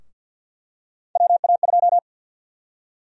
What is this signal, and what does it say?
Morse code, heard as a keyed tone, about 46 wpm, 700 Hz. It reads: WA2
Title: USA Call Sign Prefixes - Repeated 2x - 50wpm